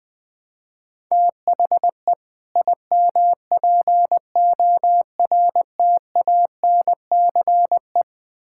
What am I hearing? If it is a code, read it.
Morse code, heard as a keyed tone, about 20 wpm, 700 Hz. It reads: THE IMPORTANCE
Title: Build A Phrase - 3-Word Phrases - 20wpm